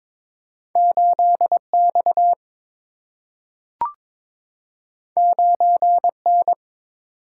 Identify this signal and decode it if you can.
Morse code, heard as a keyed tone, about 22 wpm, 700 Hz. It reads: 8 X 9 N
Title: Single-Letter Single-Number 22wpm